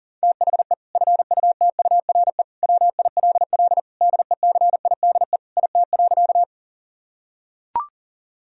Morse code, heard as a keyed tone, about 40 wpm, 700 Hz. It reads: THE FUTURE WILL DECIDE IT.
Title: Sentences from Top 1000 Words - Review Part 3 - 40wpm